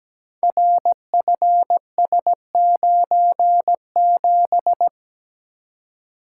Morse code, heard as a keyed tone, about 17 wpm, 700 Hz. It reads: RFS97
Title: Code Groups - 5 Letter:Numbers - 17wpm